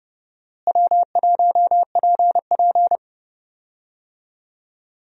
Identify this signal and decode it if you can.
Morse code, heard as a keyed tone, about 30 wpm, 700 Hz. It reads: W1PP
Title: USA Call Signs - 4 Characters - Repeated 2x - 30wpm